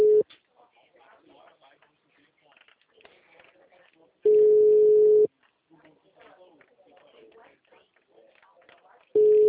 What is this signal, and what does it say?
Ringback tone (Europe, 425 Hz), with speech.